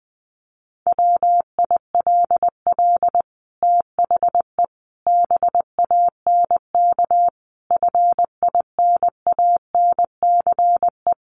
Morse code, heard as a keyed tone, about 20 wpm, 700 Hz. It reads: WILL THE BANK FINANCE
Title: Sentences from Top 1500 Words 20wpm